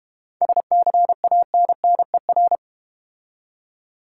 Morse code, heard as a keyed tone, about 31 wpm, 700 Hz. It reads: SCANNER